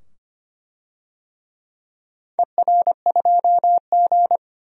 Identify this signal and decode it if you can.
Morse code, heard as a keyed tone, about 25 wpm, 700 Hz. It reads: ER2G
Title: DXCC-100 Call Signs - 4 Characters - 25wpm